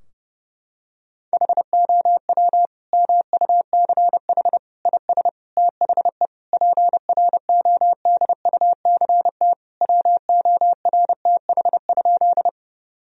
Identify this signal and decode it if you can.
Morse code, heard as a keyed tone, about 30 wpm, 700 Hz. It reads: HOW MUCH IS THE PRODUCT WORTH?
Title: Sentences from Top 1000 Words - Review Part 2 - 30wpm